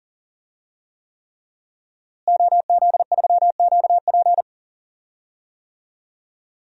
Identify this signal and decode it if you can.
Morse code, heard as a keyed tone, about 40 wpm, 700 Hz. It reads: OZ3QP
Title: DXCC-100 Call Signs - 5 Characters - Repeated 3x - 40wpm